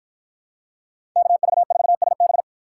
Morse code, heard as a keyed tone, about 47 wpm, 700 Hz. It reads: KV4SB